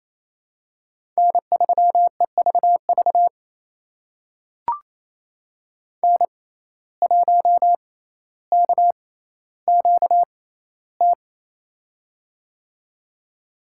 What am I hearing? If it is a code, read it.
Morse code, heard as a keyed tone, about 27 wpm, spaced as for 14 wpm, 700 Hz. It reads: N3EVV N1KQT
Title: USA Call Signs - 5 Characters - 6x Character Spacing - 28wpm